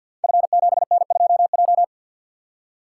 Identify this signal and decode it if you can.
Morse code, heard as a keyed tone, about 52 wpm, 700 Hz. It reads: F7N2J
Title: Code Groups - 5 Letter:Numbers - 50wpm